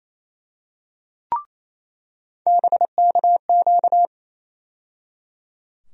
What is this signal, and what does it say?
Morse code, heard as a keyed tone, about 28 wpm, 700 Hz. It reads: BKQ
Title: USA Call Sign Postfixes - 3 Letters 28wpm